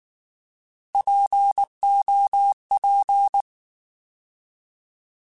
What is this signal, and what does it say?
Morse code, heard as a keyed tone, about 19 wpm, 800 Hz. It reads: POP
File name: Alphabetical / Alphabetical mp3s / Ps @18